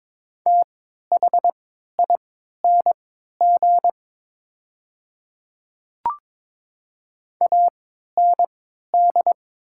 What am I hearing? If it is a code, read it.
Morse code, heard as a keyed tone, about 22 wpm, spaced as for 13 wpm, 700 Hz. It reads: THING AND
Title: 3x Character Spacing - Top 300 Words - 22wpm